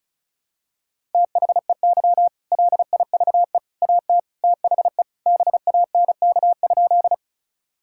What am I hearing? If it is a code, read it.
Morse code, heard as a keyed tone, about 36 wpm, 700 Hz. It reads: THEY LIVE AT THE BANK?
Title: Sentences from Top 1000 Words - Review Part 6 - 35wpm